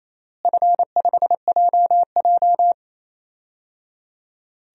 Morse code, heard as a keyed tone, about 28 wpm, 700 Hz. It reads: F5JJ
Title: DXCC-100 Call Signs - 4 Characters - Repeated 3x - 28wpm